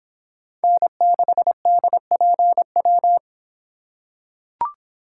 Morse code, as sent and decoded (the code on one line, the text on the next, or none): -. -.... -.. .--. .--
N6DPW